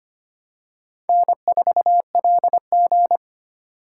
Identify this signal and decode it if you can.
Morse code, heard as a keyed tone, about 25 wpm, 700 Hz. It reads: N4LG
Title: USA Call Signs - 4 Characters - 2x Character Spacing - 25wpm